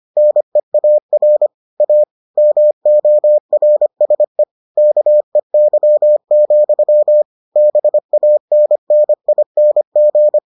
Morse code, heard as a keyed tone, about 25 wpm, 600 Hz. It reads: NEAR A MORSE KEY, BANNING